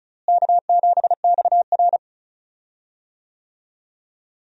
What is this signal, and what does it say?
Morse code, heard as a keyed tone, about 36 wpm, 700 Hz. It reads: K7XR
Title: USA Call Signs - 4 Characters - Repeated 2x - 35wpm